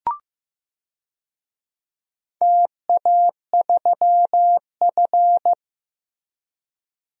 Morse code, heard as a keyed tone, about 15 wpm, 700 Hz. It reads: TA3F
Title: DXCC-100 Call Signs - 4 Characters - Repeated 2x - 15wpm